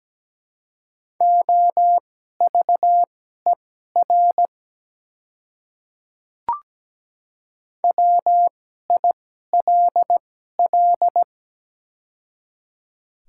Morse code, heard as a keyed tone, about 17 wpm, spaced as for 12 wpm, 700 Hz. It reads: OVER WILL
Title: 2x Character Spacing - Top 100 Words - 17wpm